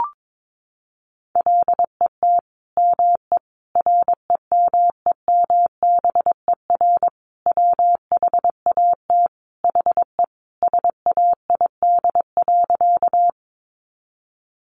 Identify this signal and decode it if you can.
Morse code, heard as a keyed tone, about 22 wpm, 700 Hz. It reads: LET ME REMEMBER WHAT HE SAID.
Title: Sentences from Top 400 Words 22wpm